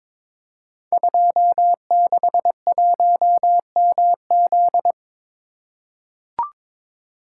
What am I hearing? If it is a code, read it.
Morse code, heard as a keyed tone, about 23 wpm, 700 Hz. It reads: 261MZ
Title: Code Groups - 5 Letter:Numbers - 22wpm